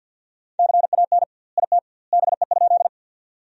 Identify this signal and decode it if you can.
Morse code, heard as a keyed tone, about 50 wpm, 700 Hz. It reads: CAN IT BE?
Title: 2x Word-Spacing - Sentences from Top 100 Words - 50wpm